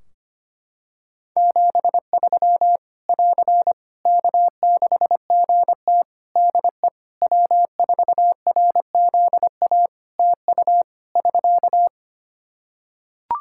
Morse code, heard as a keyed tone, about 25 wpm, 700 Hz. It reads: 73 <AR> K6GT DE W4RZA TU <SK>
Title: Practice QSO - Exchange 4 - Ending - 25wpm